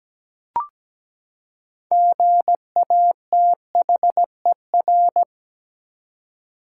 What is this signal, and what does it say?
Morse code, heard as a keyed tone, about 17 wpm, 700 Hz. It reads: GATHER
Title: Six Letter Words - 17wpm